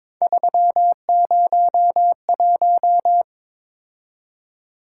Morse code, heard as a keyed tone, about 22 wpm, 700 Hz. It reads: 301
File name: Three Letter-Number Combinations - 22wpm